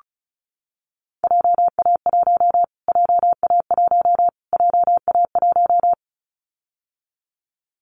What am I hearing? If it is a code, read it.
Morse code, heard as a keyed tone, about 34 wpm, 700 Hz. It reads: JA1 JA1 JA1
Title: Callsign Prefixes for Top 25 DXCC - Repeated 3x - 35wpm